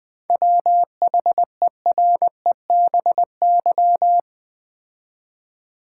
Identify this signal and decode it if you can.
Morse code, heard as a keyed tone, about 20 wpm, 700 Hz. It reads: WHEREBY
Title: 7 Letter Words - First 3 Letters Given - 20wpm